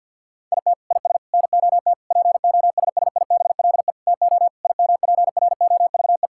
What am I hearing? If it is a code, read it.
Morse code, heard as a keyed tone, about 50 wpm, 700 Hz. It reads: IT IS NOT POSSIBLE TO IMPROVE
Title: Sentences from Top 1000 Words - Review Part 5 - 50wpm